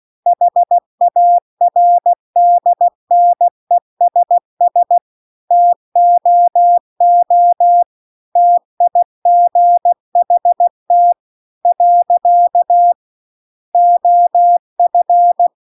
Morse code, heard as a keyed tone, about 16 wpm, 700 Hz. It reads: HARDNESS TOO TIGHT . OF